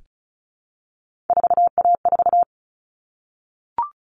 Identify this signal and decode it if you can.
Morse code, heard as a keyed tone, about 34 wpm, 700 Hz. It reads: 4A4
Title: All Cut Numbers - 3-Digit 35wpm